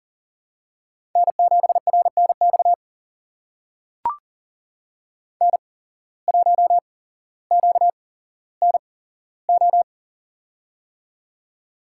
Morse code, heard as a keyed tone, about 40 wpm, spaced as for 17 wpm, 700 Hz. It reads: N7RNX N1QNO
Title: USA Call Signs - 5 Characters - 8x Character Spacing - 40wpm